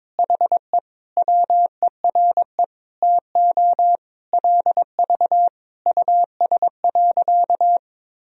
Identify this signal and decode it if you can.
Morse code, heard as a keyed tone, about 22 wpm, 700 Hz. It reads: HE WERE TO LV US.
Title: Sentences from Top 400 Words and Abbreviations 22wpm